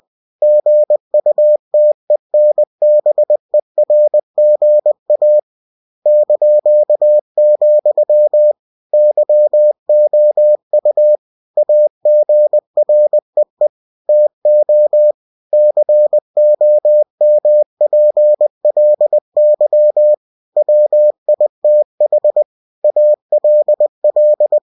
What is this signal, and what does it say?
Morse code, heard as a keyed tone, about 20 wpm, 600 Hz. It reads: GUTENBERGA ), YOU AGREE TO COMPLY WITH ALL
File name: cw-books / winnie-the-pooh / winnie-20wpm-page-0113